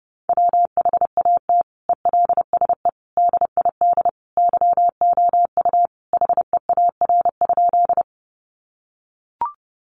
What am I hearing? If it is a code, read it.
Morse code, heard as a keyed tone, about 28 wpm, 700 Hz. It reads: WHAT ELSE DID YOU HEAR?
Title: Sentences from Top 800 Words 30wpm